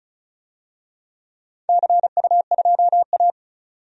Morse code, heard as a keyed tone, about 36 wpm, 700 Hz. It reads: CU2A
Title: DXCC-100 Call Signs - 4 Characters - 35wpm